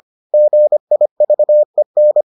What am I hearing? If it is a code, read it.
Morse code, heard as a keyed tone, about 25 wpm, 600 Hz. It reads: GIVEN